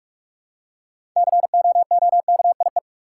Morse code, heard as a keyed tone, about 46 wpm, 700 Hz. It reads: COOKIE
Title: Six Letter Words - 45wpm